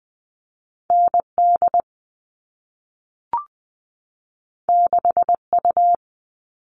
Morse code, heard as a keyed tone, about 20 wpm, 700 Hz. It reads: ND 6U
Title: All Cut Numbers - 2-Digit 20wpm